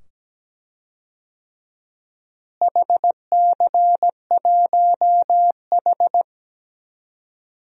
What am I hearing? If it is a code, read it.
Morse code, heard as a keyed tone, about 17 wpm, 700 Hz. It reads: HC1H
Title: DXCC-100 Call Signs - 4 Characters - Repeated 2x - 17wpm